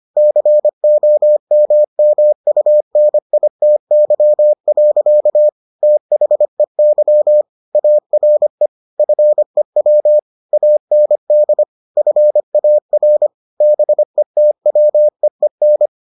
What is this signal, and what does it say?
Morse code, heard as a keyed tone, about 25 wpm, 600 Hz. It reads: COMMUNITY. THEY ARE FEW AND FAR BETWEEN